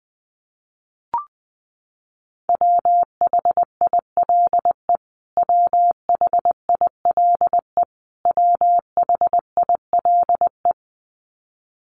Morse code, heard as a keyed tone, about 20 wpm, 700 Hz. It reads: WHILE WHILE WHILE
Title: Top 500 Words Repeated 20wpm